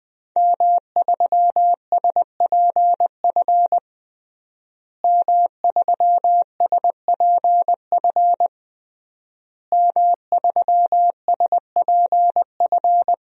Morse code, heard as a keyed tone, about 20 wpm, 700 Hz. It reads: M3SPF M3SPF M3SPF
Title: DXCC-100 Call Signs - 5 Characters - Repeated 3x - 20wpm